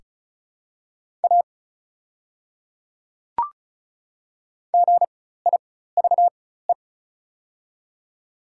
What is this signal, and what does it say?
Morse code, heard as a keyed tone, about 36 wpm, spaced as for 16 wpm, 700 Hz. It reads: A GIVE
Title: 4x Character Spacing - Top 200 Words - 35wpm